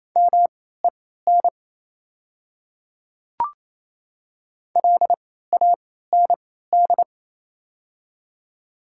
Morse code, heard as a keyed tone, about 27 wpm, spaced as for 16 wpm, 700 Hz. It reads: MEN LAND
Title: 3x Character Spacing - Top 200 Words - 28wpm